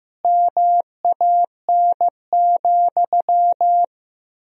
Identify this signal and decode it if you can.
Morse code, heard as a keyed tone, about 15 wpm, 700 Hz. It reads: MAN,
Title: 2x Word-Spacing - Sentences from Top 400 Words - 15wpm